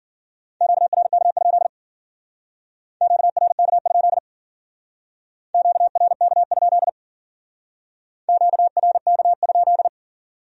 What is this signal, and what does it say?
Morse code, heard as a keyed tone, about 50 wpm, 700 Hz. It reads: QRK? QRK? QRK? QRK?
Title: Speed-Racing Q-Codes - 40wpm